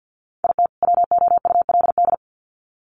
Morse code, heard as a keyed tone, about 50 wpm, 700 Hz. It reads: IT WOULD
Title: Sets of 2 Words 50wpm